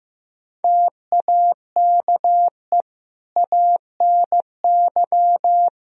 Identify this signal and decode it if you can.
Morse code, heard as a keyed tone, about 15 wpm, 700 Hz. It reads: TAKE ANY
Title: Sentences from Top 1000 Words - Review Part 3 - 15wpm